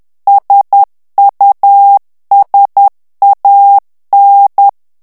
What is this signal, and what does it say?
Morse code, heard as a keyed tone, about 10 wpm, 800 Hz. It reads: SUSAN
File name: morse mp3 / ITALIEN texte clair 1 vitesse 10 mots minute